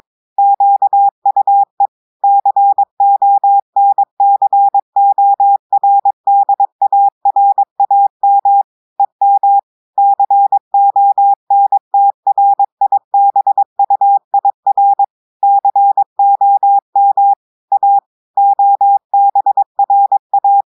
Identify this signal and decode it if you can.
Morse code, heard as a keyed tone, about 23 wpm, 800 Hz. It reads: QUE CONCORDARAM EM CONTRIBUIR COM A OBRA
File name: audios / portugais / portugais texte en clair 1 vitesse 22 mots minutes